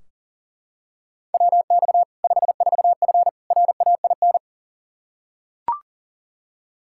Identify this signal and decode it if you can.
Morse code, heard as a keyed tone, about 40 wpm, 700 Hz. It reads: WX 54F RAIN
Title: Practice QSO - Exchange 2 - Info - 40wpm